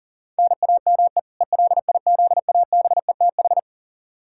Morse code, heard as a keyed tone, about 40 wpm, 700 Hz. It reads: NAME ELIZABETH
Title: K1USN SST QSO Practice 40wpm